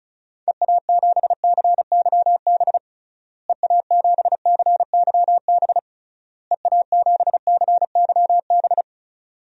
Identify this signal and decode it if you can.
Morse code, heard as a keyed tone, about 36 wpm, 700 Hz. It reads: EA7CYB EA7CYB EA7CYB
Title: DXCC-100 Call Signs - 6 Characters - Repeated 3x - 35wpm